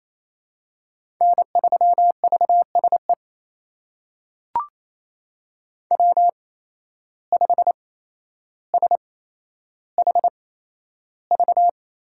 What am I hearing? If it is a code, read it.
Morse code, heard as a keyed tone, about 28 wpm, spaced as for 12 wpm, 700 Hz. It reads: N3VSE W5SHV
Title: USA Call Signs - 5 Characters - 8x Character Spacing - 28wpm